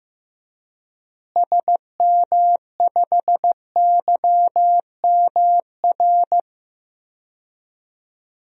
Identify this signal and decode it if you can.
Morse code, heard as a keyed tone, about 15 wpm, 700 Hz. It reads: SM5YMR
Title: DXCC-100 Call Signs - 6 Characters - Repeated 3x - 15wpm